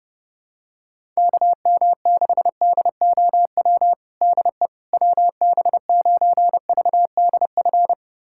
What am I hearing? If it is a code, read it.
Morse code, heard as a keyed tone, about 30 wpm, 700 Hz. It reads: KM6DOW DE WB9VDF
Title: USA Callsign DE Callsign - 6 Characters - 30wpm